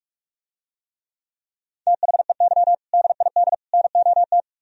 Morse code, heard as a keyed tone, about 46 wpm, 700 Hz. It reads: THEY DID NOT